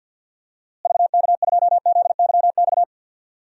Morse code, heard as a keyed tone, about 52 wpm, 700 Hz. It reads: VK2ZYX
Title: DXCC-100 Call Signs - 6 Characters - Repeated 3x - 50wpm